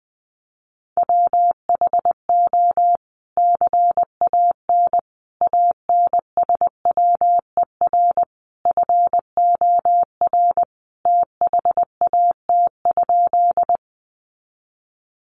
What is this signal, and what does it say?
Morse code, heard as a keyed tone, about 20 wpm, 700 Hz. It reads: WHO CAN ANSWER FOR THAT?
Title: Sentences from Top 300 Words 20wpm